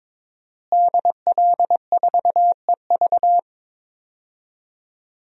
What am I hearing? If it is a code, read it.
Morse code, heard as a keyed tone, about 22 wpm, 700 Hz. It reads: DL4EV
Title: DXCC-100 Call Signs - 5 Characters - Repeated 3x - 22wpm